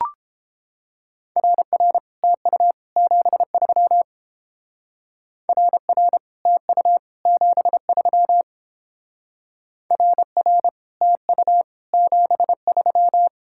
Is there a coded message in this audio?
Yes: Morse code from 1.4 to 13.3 s, heard as a keyed tone, about 29 wpm, 700 Hz. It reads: RR TU 73 RR TU 73 RR TU 73